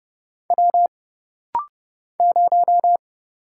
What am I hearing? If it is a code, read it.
Morse code, heard as a keyed tone, about 30 wpm, 700 Hz. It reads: W 0
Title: Single Letter-Number Alt - Rapid-Fire 30wpm